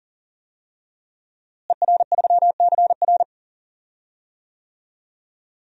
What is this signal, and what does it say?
Morse code, heard as a keyed tone, about 40 wpm, 700 Hz. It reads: ER3CR
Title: DXCC-100 Call Signs - 5 Characters - Repeated 2x - 40wpm